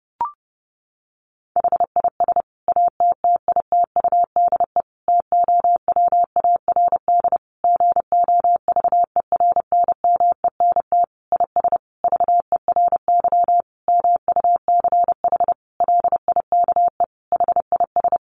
Morse code, heard as a keyed tone, about 30 wpm, 700 Hz. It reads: HIS ATTITUDE TOWARD GOVERNMENT IS VERY MUCH LIKE HIS
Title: Sentences from Top 1300 Words 30wpm